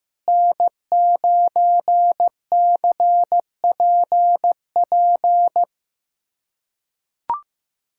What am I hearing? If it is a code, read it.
Morse code, heard as a keyed tone, about 16 wpm, 700 Hz. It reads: N9CPP
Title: USA Callsign DE Callsign - 5 Characters - 15wpm